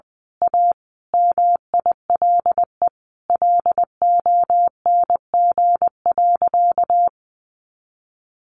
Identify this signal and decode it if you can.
Morse code, heard as a keyed tone, about 20 wpm, 700 Hz. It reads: A MILE LONG.
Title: Sentences from Top 500 Words 20wpm